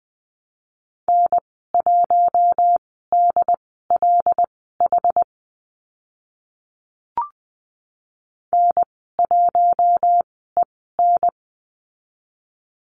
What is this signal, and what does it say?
Morse code, heard as a keyed tone, about 20 wpm, spaced as for 14 wpm, 700 Hz. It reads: N1DLH N1EN
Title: 082 Lesson 14g - Number 2 - Call Signs 20wpm